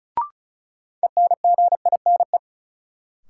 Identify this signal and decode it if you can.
Morse code, heard as a keyed tone, about 36 wpm, 700 Hz. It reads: ENGINE
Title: Sentences from Top 1500 Words - New Word Drill - 35wpm